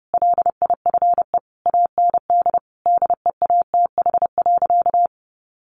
Morse code, heard as a keyed tone, about 30 wpm, 700 Hz. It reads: LIFE AND DEATH.
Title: Sentences from Top 1100 Words 30wpm